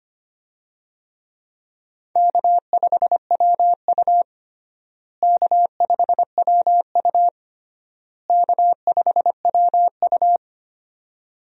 Morse code, heard as a keyed tone, about 25 wpm, 700 Hz. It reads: K5WU K5WU K5WU
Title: USA Call Signs - 4 Characters - Repeated 3x - 25wpm